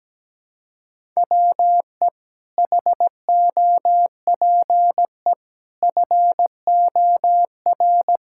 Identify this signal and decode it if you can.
Morse code, heard as a keyed tone, about 17 wpm, 700 Hz. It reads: WE HOPE FOR
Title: Sentences from Top 1000 Words - Review Part 6 - 17wpm